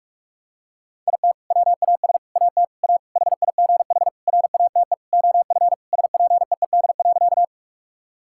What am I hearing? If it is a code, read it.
Morse code, heard as a keyed tone, about 46 wpm, 700 Hz. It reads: IT WAS AT A HIGH RATE OF SPEED.
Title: Sentences from Top 1000 Words - Review Part 5 - 45wpm